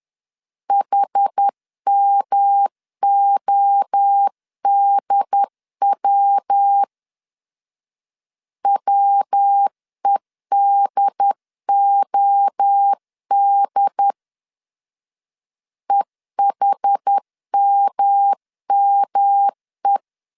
Morse code, heard as a keyed tone, about 10 wpm, 800 Hz. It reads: HMODW WEDOD EHMME
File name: 10wpm